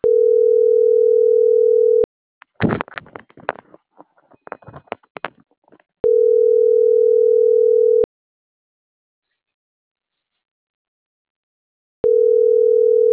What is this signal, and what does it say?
Ringback tone (North America), 440 + 480 Hz.
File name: CENTRAL NMP / out marisol.monzalvonmp